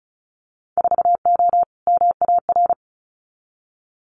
Morse code, heard as a keyed tone, about 34 wpm, 700 Hz. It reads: 4O MAR
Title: Winter Field Day Exchanges Encore 35wpm